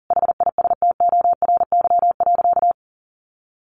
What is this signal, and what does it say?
Morse code, heard as a keyed tone, about 40 wpm, 700 Hz. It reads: HISTORY.
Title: Sentences from Top 1300 Words 40wpm